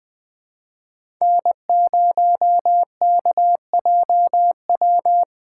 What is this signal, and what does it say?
Morse code, heard as a keyed tone, about 20 wpm, 700 Hz. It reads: N0KJW
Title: USA Call Signs - 5 Characters - 4x Character Spacing - 20wpm